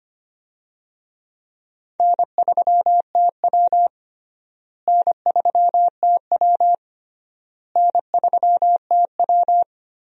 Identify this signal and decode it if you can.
Morse code, heard as a keyed tone, about 25 wpm, 700 Hz. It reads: N3TW N3TW N3TW
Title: USA Call Signs - 4 Characters - Repeated 3x - 25wpm